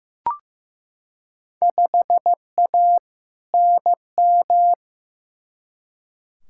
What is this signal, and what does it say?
Morse code, heard as a keyed tone, about 15 wpm, 700 Hz. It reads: 5A NM
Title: Field Day Exchanges - 15wpm